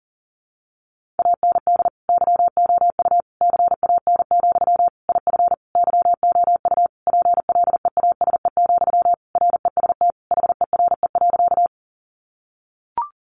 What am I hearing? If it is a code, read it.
Morse code, heard as a keyed tone, about 40 wpm, 700 Hz. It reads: AND YOU CAN, IF YOU PLEASE, REST HERE.
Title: Sentences from Top 1500 Words 40wpm